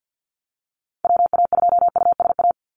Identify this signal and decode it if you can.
Morse code, heard as a keyed tone, about 53 wpm, 700 Hz. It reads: WA2USA
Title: NA SOTA Activator Callsigns 50wpm